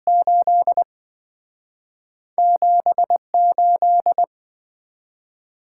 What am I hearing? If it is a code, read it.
Morse code, heard as a keyed tone, about 26 wpm, 700 Hz. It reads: 8 78